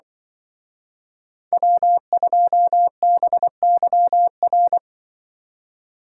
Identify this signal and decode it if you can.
Morse code, heard as a keyed tone, about 24 wpm, 700 Hz. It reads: W2BYR
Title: Speed Racing - USA 1x3 Call Signs 20wpm